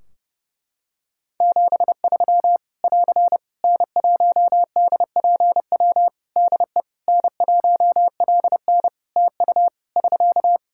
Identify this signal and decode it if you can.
Morse code, heard as a keyed tone, about 29 wpm, 700 Hz. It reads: 73 <AR> N1DPW DE N1LN TU <SK>
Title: Practice QSO - Exchange 4 - Ending - 30wpm